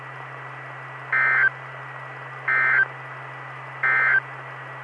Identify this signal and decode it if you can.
Emergency Alert System (SAME) tones, heard as harsh buzzing bursts. It is heard from 1.0 to 4.3 s.
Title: Saline, AK SKYWARN Repeater System - W5RHS